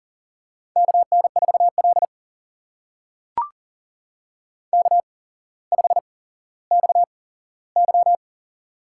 Morse code, heard as a keyed tone, about 42 wpm, spaced as for 17 wpm, 700 Hz. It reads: KN4L K5XY
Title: USA Call Signs - 4 Characters - 8x Character Spacing - 40wpm